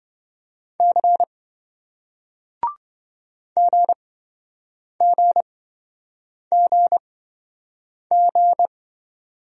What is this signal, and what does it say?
Morse code, heard as a keyed tone, about 26 wpm, spaced as for 7 wpm, 700 Hz. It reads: C GGGG